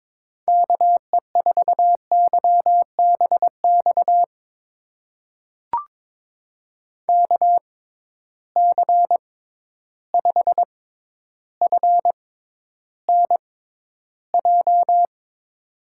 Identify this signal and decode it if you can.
Morse code, heard as a keyed tone, about 22 wpm, spaced as for 11 wpm, 700 Hz. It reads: KE4YBX KC5FNJ
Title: USA Call Signs - 6 Characters - 6x Character Spacing - 22wpm